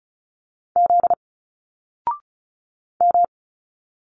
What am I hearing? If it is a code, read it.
Morse code, heard as a keyed tone, about 35 wpm, 700 Hz. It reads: Z M
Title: Single Letters 35wpm